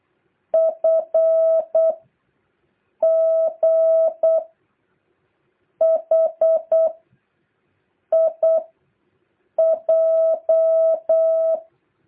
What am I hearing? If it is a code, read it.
Morse code, heard as a keyed tone, about 8 wpm, 660 Hz. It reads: F G H I J
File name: jh8cbh / 2023cw